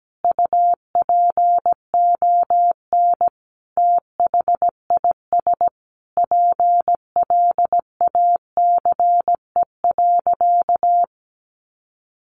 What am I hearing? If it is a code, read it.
Morse code, heard as a keyed tone, about 17 wpm, 700 Hz. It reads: UPON THIS PLACE.
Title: Sentences from Top 1100 Words 17wpm